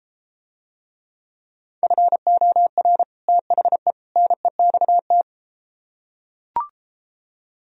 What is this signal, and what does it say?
Morse code, heard as a keyed tone, about 33 wpm, 700 Hz. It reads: FOR THE NEXT